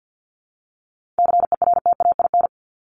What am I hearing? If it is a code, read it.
Morse code, heard as a keyed tone, about 50 wpm, 700 Hz. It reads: CERTAIN
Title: Top 500 Words 50wpm